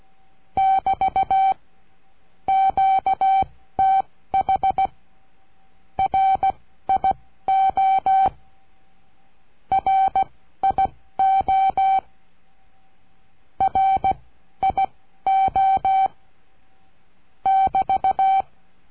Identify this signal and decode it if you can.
Morse code, heard as a keyed tone, about 16 wpm, spaced as for 13 wpm, 770 Hz. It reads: <BT> QTH RIO RIO RIO <BT>
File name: cursos / CURSO DE TELEGRAFIA / Telegrafia - Trilha 22